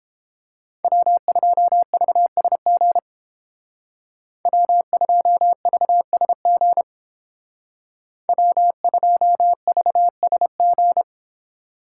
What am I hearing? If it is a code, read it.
Morse code, heard as a keyed tone, about 31 wpm, 700 Hz. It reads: W2VSG W2VSG W2VSG